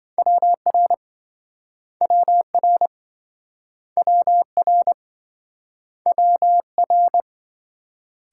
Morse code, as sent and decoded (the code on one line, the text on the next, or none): .-- .-. / .-- .-. / .-- .-. / .-- .-.
WR WR WR WR